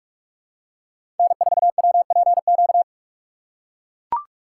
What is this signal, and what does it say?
Morse code, heard as a keyed tone, about 41 wpm, 700 Hz. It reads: N4WPQ
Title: USA Call Signs - 5 Characters - 4x Character Spacing - 45wpm